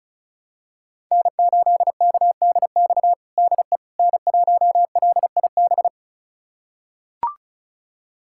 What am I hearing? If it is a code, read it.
Morse code, heard as a keyed tone, about 34 wpm, 700 Hz. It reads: N8KDX DE N1LIB
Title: USA Callsign DE Callsign - 5 Characters - 35wpm